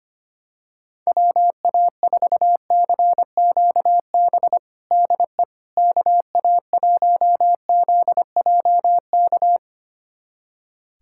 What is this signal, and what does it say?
Morse code, heard as a keyed tone, about 25 wpm, 700 Hz. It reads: WA4CQB DE KA1ZJK
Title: USA Callsign DE Callsign - 6 Characters - 25wpm